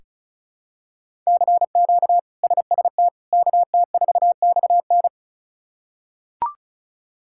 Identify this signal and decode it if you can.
Morse code, heard as a keyed tone, about 36 wpm, 700 Hz. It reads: CQ SST KT4XN
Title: K1USN SST QSO Practice - 35wpm